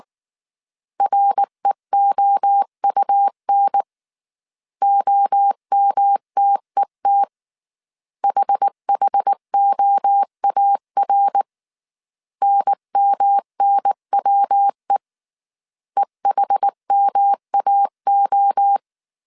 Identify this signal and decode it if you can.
Morse code, heard as a keyed tone, about 19 wpm, 800 Hz. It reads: REOUN OMTET HHOAR NMNWE EHMAO
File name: CW / DET / 1 3 1500C S B 18wpm 1916